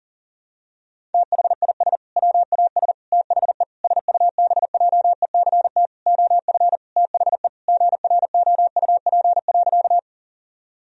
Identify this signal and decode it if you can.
Morse code, heard as a keyed tone, about 40 wpm, 700 Hz. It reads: THIS WAS THE SUBJECT OF THE GROUP.
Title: Sentences from Top 1000 Words - Review Part 1 - 40wpm